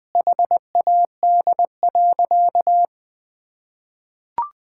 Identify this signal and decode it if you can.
Morse code, heard as a keyed tone, about 21 wpm, 700 Hz. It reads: HAD.
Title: Sentences from Top 1000 Words - Review Part 6 - 20wpm